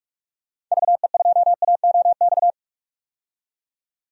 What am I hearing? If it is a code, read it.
Morse code, heard as a keyed tone, about 46 wpm, 700 Hz. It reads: VE2AOX
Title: DXCC-100 Call Signs - 6 Characters - 45wpm